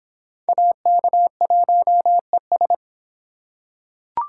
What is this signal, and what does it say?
Morse code, heard as a keyed tone, about 22 wpm, 700 Hz. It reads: AK1ES